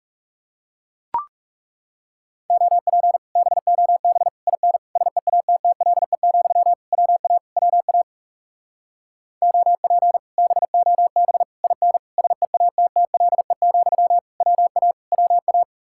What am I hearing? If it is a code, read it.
Morse code, heard as a keyed tone, about 37 wpm, 700 Hz. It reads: OP BOB IN SEATTLE, WA WA OP BOB IN SEATTLE, WA WA
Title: Speed-Racing Common QSO Phrases Encore - 30wpm